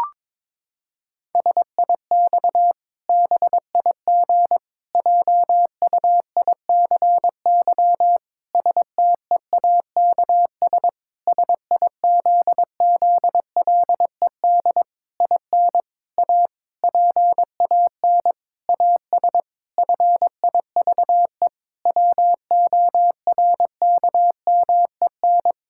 Morse code, heard as a keyed tone, about 22 wpm, 700 Hz. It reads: SIX BIG JUICY STEAKS SIZZLED IN A PAN AS FIVE WORKMEN
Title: Pangram Challenge v2 - 22wpm